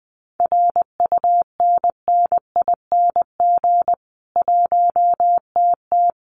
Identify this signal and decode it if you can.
Morse code, heard as a keyed tone, about 20 wpm, 700 Hz. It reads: RUNNING 1TT
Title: Common QSO Phrases 20wpm